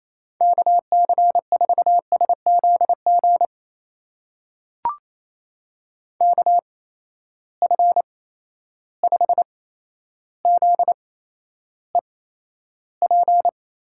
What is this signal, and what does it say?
Morse code, heard as a keyed tone, about 28 wpm, spaced as for 12 wpm, 700 Hz. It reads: KC4SZG KF5ZEP
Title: USA Call Signs - 6 Characters - 8x Character Spacing - 28wpm